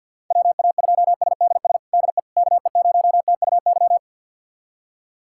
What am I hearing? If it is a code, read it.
Morse code, heard as a keyed tone, about 52 wpm, 700 Hz. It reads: WA2SDS DE KE0TUY
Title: USA Callsign DE Callsign - 6 Characters - 50wpm